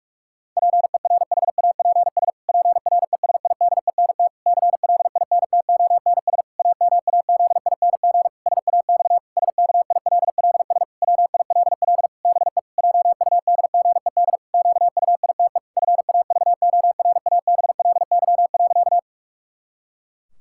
Morse code, heard as a keyed tone, about 46 wpm, 700 Hz. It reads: PERHAPS PRESIDENT CLINTONS AMAZING SAX SKILLS WILL BE JUDGED QUITE FAVORABLY.
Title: Pangram Challenge v2 - 45wpm